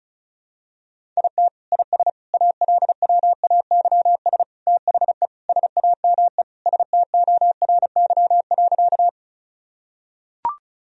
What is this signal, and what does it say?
Morse code, heard as a keyed tone, about 34 wpm, 700 Hz. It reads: IT IS ALWAYS THE SAME STORY.
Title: Sentences from Top 1000 Words - Review Part 1 - 35wpm